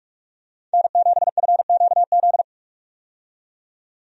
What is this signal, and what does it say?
Morse code, heard as a keyed tone, about 46 wpm, 700 Hz. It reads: N7FQZ
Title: USA Call Signs - 5 Characters - 45wpm